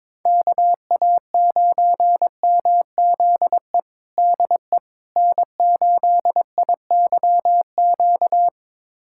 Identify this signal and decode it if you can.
Morse code, heard as a keyed tone, about 22 wpm, 700 Hz. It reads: KA9MZE DE N8IYQ
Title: USA Callsign DE Callsign 22wpm